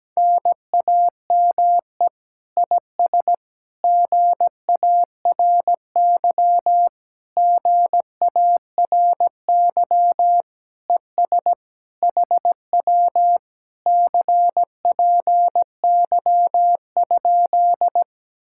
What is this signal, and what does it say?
Morse code, heard as a keyed tone, about 17 wpm, 700 Hz. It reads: NAME IS GARY GARY ES HW CPY?
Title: Speed-Racing Common QSO Phrases Encore - 17wpm